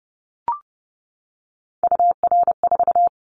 Morse code, heard as a keyed tone, about 30 wpm, 700 Hz. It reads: UR4
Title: Callsign Prefixes for Top 25 DXCC - 30wpm